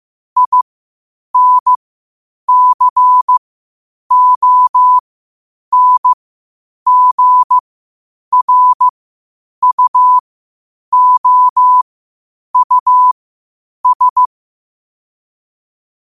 Morse code, heard as a keyed tone, about 15 wpm, spaced as for 8 wpm, 1000 Hz. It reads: INCONGRUOUS
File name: qotdfiles / 8wpm